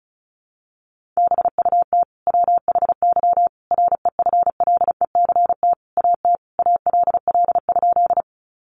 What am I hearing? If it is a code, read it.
Morse code, heard as a keyed tone, about 34 wpm, 700 Hz. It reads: BUT WHY REFLECT AT ALL?
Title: Sentences from Top 1500 Words 35wpm